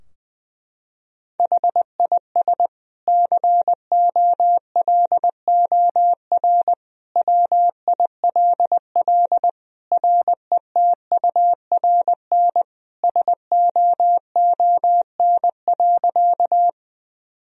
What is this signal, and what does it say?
Morse code, heard as a keyed tone, about 20 wpm, 700 Hz. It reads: HIS COLOR WILL RETURN SOON.
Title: Sentences from Top 1000 Words - Review Part 2 - 20wpm